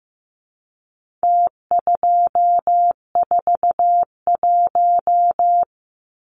Morse code, heard as a keyed tone, about 15 wpm, 700 Hz. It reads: T241
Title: Numbers - Contest Style - 4-Digit 15wpm